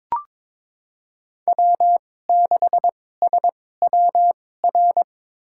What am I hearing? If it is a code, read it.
Morse code, heard as a keyed tone, about 22 wpm, spaced as for 16 wpm, 700 Hz. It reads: W6SWR
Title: USA Call Signs - 5 Characters - 2x Character Spacing - 22wpm